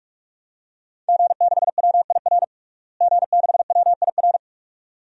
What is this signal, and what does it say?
Morse code, heard as a keyed tone, about 44 wpm, 700 Hz. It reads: G6WIR G6WIR
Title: DXCC-100 Call Signs - 5 Characters - Repeated 2x - 45wpm